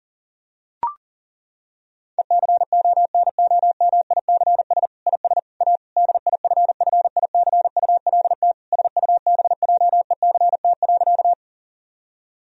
Morse code, heard as a keyed tone, about 40 wpm, 700 Hz. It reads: ECONOMICS IS A DIFFICULT SUBJECT.
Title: Sentences from Top 1000 Words - Review Part 3 - 40wpm